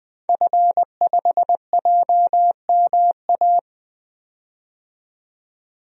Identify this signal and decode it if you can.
Morse code, heard as a keyed tone, about 20 wpm, 700 Hz. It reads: F5JMA
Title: DXCC-100 Call Signs - 5 Characters - 20wpm